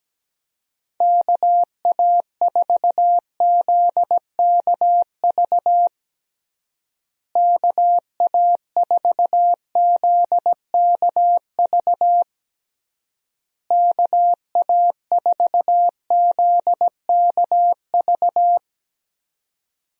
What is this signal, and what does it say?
Morse code, heard as a keyed tone, about 17 wpm, 700 Hz. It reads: KA4ZKV KA4ZKV KA4ZKV
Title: USA Call Signs - 6 Characters - Repeated 3x - 17wpm